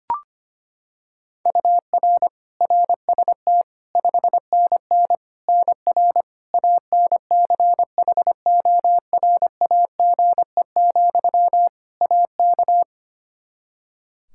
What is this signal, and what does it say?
Morse code, heard as a keyed tone, about 25 wpm, 700 Hz. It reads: UR RST 5NN NR ANCHORAGE, AK
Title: Practice QSO - Exchange 1 - Beginning - 25wpm